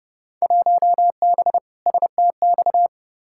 Morse code, heard as a keyed tone, about 29 wpm, 700 Hz. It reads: 1B STX
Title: Field Day Exchanges Encore - 30wpm